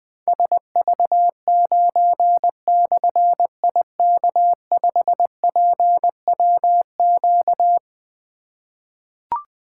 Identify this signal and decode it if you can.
Morse code, heard as a keyed tone, about 20 wpm, 700 Hz. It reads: SV9/IK5PWQ
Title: Difficult Call Signs 20wpm